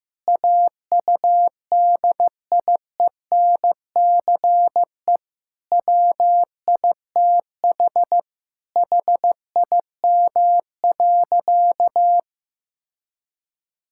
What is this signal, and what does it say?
Morse code, heard as a keyed tone, about 14 wpm, 700 Hz. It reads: AUDIENCE WITH HIM.
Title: Sentences from Top 1000 Words - Review Part 5 - 15wpm